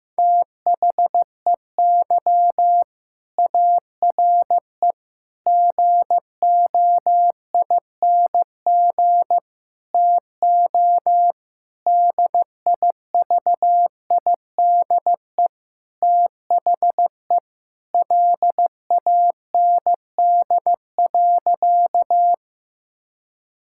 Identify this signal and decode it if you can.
Morse code, heard as a keyed tone, about 15 wpm, 700 Hz. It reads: THEY ARE GOING TO DIVIDE THE LAND.
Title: Sentences from Top 1600 Words 15wpm